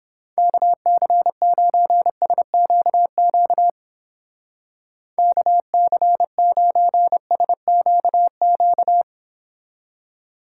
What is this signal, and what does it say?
Morse code, heard as a keyed tone, about 28 wpm, 700 Hz. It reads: KC9SQQ KC9SQQ
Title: Speed Racing - USA 2x3 Call Signs 22wpm